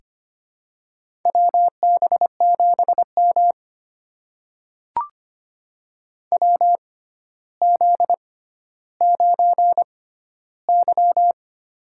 Morse code, heard as a keyed tone, about 25 wpm, spaced as for 13 wpm, 700 Hz. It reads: WB7M WZ9Y
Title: USA Call Signs - 4 Characters - 6x Character Spacing - 25wpm